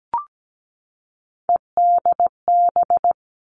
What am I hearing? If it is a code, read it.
Morse code, heard as a keyed tone, about 17 wpm, 700 Hz. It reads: EDB
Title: Three Challenging Letter-Number Combinations 17wpm